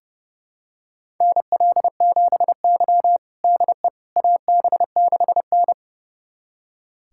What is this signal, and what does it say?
Morse code, heard as a keyed tone, about 30 wpm, 700 Hz. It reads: NL7Y DE AB6N
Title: USA Callsign DE Callsign - 4 Characters - 30wpm